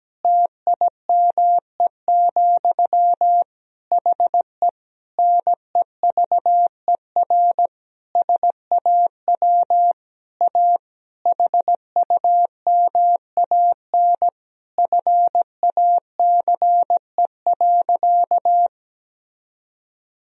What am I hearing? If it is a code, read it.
Morse code, heard as a keyed tone, about 17 wpm, 700 Hz. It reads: TIME, HE NEVER SAW A HUMAN FACE.
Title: Sentences from Top 1000 Words - Review Part 3 - 17wpm